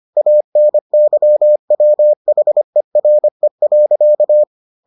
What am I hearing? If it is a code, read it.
Morse code, heard as a keyed tone, about 25 wpm, 600 Hz. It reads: ANYWHERE.